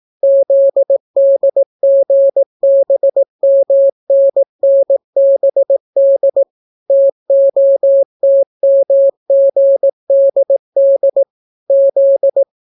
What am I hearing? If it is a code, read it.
Morse code, heard as a keyed tone, about 18 wpm, 550 Hz. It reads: ZDGBMNNBD TOTMGDD Z